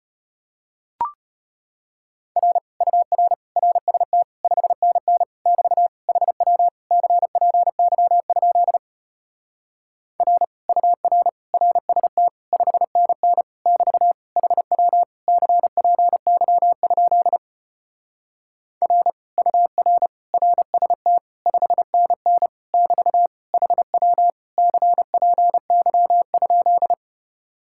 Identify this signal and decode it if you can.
Morse code, heard as a keyed tone, about 33 wpm, 700 Hz. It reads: R UR RST 5NN <BT> HW CPY? R UR RST 5NN <BT> HW CPY? R UR RST 5NN <BT> HW CPY?